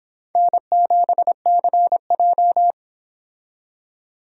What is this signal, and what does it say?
Morse code, heard as a keyed tone, about 31 wpm, 700 Hz. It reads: N7CJ